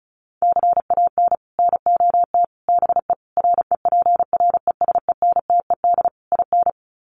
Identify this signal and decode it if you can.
Morse code, heard as a keyed tone, about 34 wpm, 700 Hz. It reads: CAN NOT BE REPRESENTED IN
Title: Sets of 5 Words 35wpm